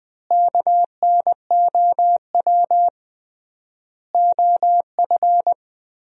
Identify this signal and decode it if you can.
Morse code, heard as a keyed tone, about 20 wpm, 700 Hz. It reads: KNOW OF
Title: 3x Word-Spacing - Sentences from Top 100 Words - 20wpm